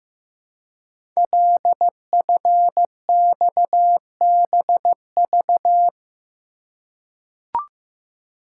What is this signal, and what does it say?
Morse code, heard as a keyed tone, about 15 wpm, 700 Hz. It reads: LFXBV
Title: Code Groups - 5 Letters - 15wpm